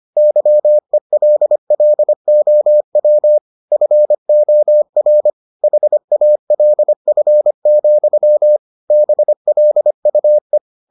Morse code, heard as a keyed tone, about 25 wpm, 600 Hz. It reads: YELLOW FOR HALF, BLUE